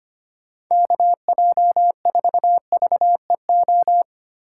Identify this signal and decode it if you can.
Morse code, heard as a keyed tone, about 25 wpm, 700 Hz. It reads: KJ4VEO
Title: USA Call Signs - 6 Characters - 2x Character Spacing - 25wpm